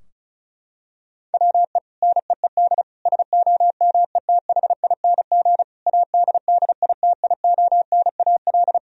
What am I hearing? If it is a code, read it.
Morse code, heard as a keyed tone, about 36 wpm, 700 Hz. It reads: WE NEED SOMETHING ADDITIONAL
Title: Sentences from Top 1000 Words - Review Part 4 - 35wpm